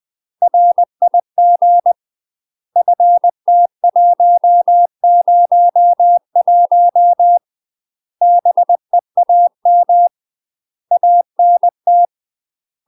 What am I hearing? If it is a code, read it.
Morse code, heard as a keyed tone, about 20 wpm, 700 Hz. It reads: RIG FT101 BEAM ANT
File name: UK MASTER QSOs TRAINING/20WPM